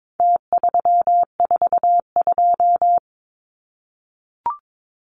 Morse code, heard as a keyed tone, about 22 wpm, 700 Hz. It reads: T342
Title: Numbers - Contest Style - 4-Digit 22wpm